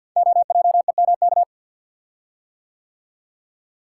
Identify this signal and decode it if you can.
Morse code, heard as a keyed tone, about 50 wpm, 700 Hz. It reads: OJEMK
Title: Code Groups - 5 Letters - 50wpm